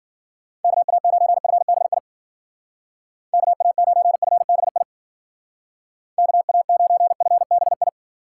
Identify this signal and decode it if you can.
Morse code, heard as a keyed tone, about 49 wpm, 700 Hz. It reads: KA9FBI KA9FBI KA9FBI
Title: Speed Racing - USA 2x3 Call Signs 40wpm